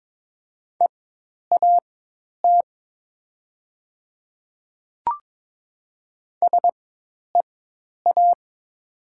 Morse code, heard as a keyed tone, about 22 wpm, spaced as for 10 wpm, 700 Hz. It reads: EAT SEA
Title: 4x Character Spacing - Top 300 Words - 22wpm